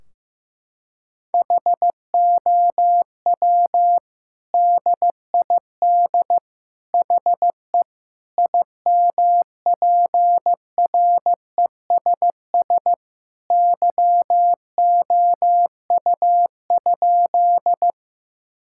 Morse code, heard as a keyed tone, about 15 wpm, 700 Hz. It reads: HOW DID HE IMPRESS YOU?
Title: Sentences from Top 1600 Words 15wpm